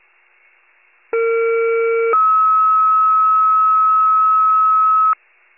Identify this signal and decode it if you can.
Two-tone page, heard as a one-second tone, then a three-second tone, about 460 then 1290 Hz. It is heard from 1.1 to 5.1 s.